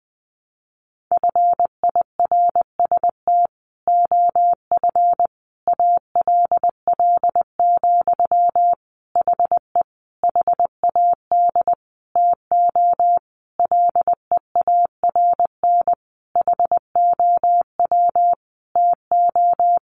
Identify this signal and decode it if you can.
Morse code, heard as a keyed tone, about 20 wpm, 700 Hz. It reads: FIRST OF ALL, HE HAD TO LEARN HOW TO
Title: Sentences from Top 1200 Words 20wpm